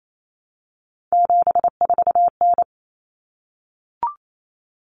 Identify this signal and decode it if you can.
Morse code, heard as a keyed tone, about 28 wpm, 700 Hz. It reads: 74N
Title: Common Cut Numbers - 3-Digit 28wpm